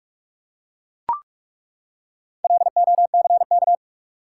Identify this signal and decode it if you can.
Morse code, heard as a keyed tone, about 42 wpm, 700 Hz. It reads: LOCK
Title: Four Letter Words - 45wpm